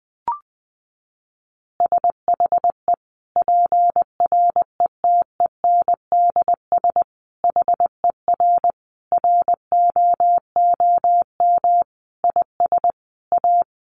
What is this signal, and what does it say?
Morse code, heard as a keyed tone, about 20 wpm, 700 Hz. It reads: SHE PRETENDS HER ROOM IS A
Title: Additional Easy Phrases 20wpm